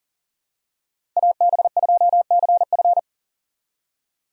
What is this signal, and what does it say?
Morse code, heard as a keyed tone, about 40 wpm, 700 Hz. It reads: AB2CF
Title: USA Call Signs - 5 Characters - 40wpm